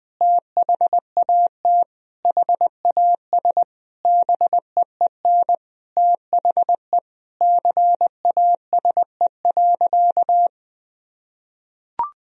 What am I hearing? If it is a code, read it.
Morse code, heard as a keyed tone, about 20 wpm, 700 Hz. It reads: THAT HAS BEEN THE CASE.
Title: Sentences from Top 1000 Words - Review Part 3 - 20wpm